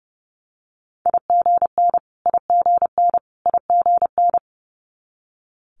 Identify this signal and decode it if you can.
Morse code, heard as a keyed tone, about 30 wpm, 700 Hz. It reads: IGN IGN IGN
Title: Multi-letter Phonemes Repeated 3x 30wpm